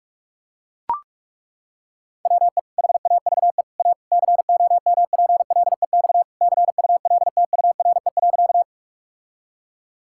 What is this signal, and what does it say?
Morse code, heard as a keyed tone, about 41 wpm, 700 Hz. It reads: WE HAVE A COMPLEX CULTURE.
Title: Sentences from Top 1000 Words - Review Part 6 - 45wpm